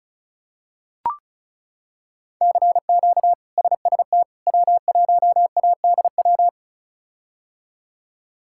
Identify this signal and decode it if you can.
Morse code, heard as a keyed tone, about 36 wpm, 700 Hz. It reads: CQ SST W1ADW
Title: K1USN SST QSO Practice - 35wpm